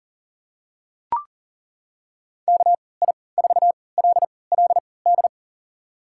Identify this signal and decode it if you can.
Morse code, heard as a keyed tone, about 40 wpm, spaced as for 23 wpm, 700 Hz. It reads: KI4LLD
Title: USA Call Signs - 6 Characters - 3x Character Spacing - 40wpm